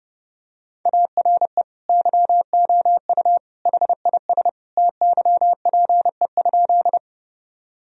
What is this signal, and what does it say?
Morse code, heard as a keyed tone, about 32 wpm, 700 Hz. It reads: ARE YOU HIS TYPE?
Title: Sentences from Top 1000 Words - Review Part 3 - 30wpm